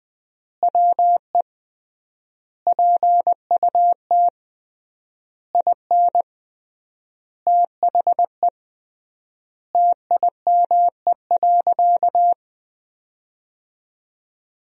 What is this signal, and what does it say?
Morse code, heard as a keyed tone, about 21 wpm, 700 Hz. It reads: WE PUT IN THE TIME.
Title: 3x Word-Spacing - Sentences from Top 200 Words - 20wpm